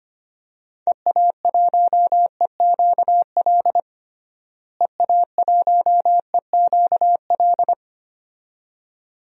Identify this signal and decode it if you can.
Morse code, heard as a keyed tone, about 25 wpm, 700 Hz. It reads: EA1EQL EA1EQL
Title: DXCC-100 Call Signs - 6 Characters - Repeated 2x - 25wpm